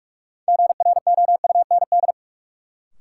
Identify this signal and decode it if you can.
Morse code, heard as a keyed tone, about 43 wpm, 700 Hz. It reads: GROUND
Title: Sets of 3 Words - New Word Drill - 45wpm